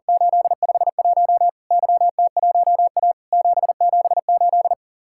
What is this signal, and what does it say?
Morse code, heard as a keyed tone, about 44 wpm, 700 Hz. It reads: 851 YT1A 778
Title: Worked All Europe DX Contest - QTC Exchange - 40wpm